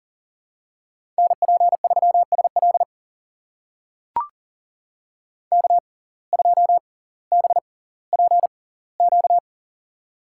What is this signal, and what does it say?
Morse code, heard as a keyed tone, about 40 wpm, spaced as for 20 wpm, 700 Hz. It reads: NP3SL K2BPQ
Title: USA Call Signs - 5 Characters - 6x Character Spacing - 40wpm